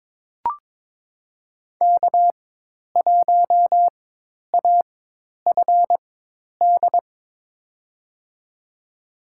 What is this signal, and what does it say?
Morse code, heard as a keyed tone, about 23 wpm, spaced as for 10 wpm, 700 Hz. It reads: K1AFD
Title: USA Call Signs - 4x Character Spacing - 22wpm